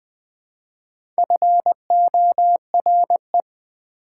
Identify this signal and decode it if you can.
Morse code, heard as a keyed tone, about 20 wpm, 700 Hz. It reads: FORE
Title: Speed-Racing Prefixes and Suffixes 17wpm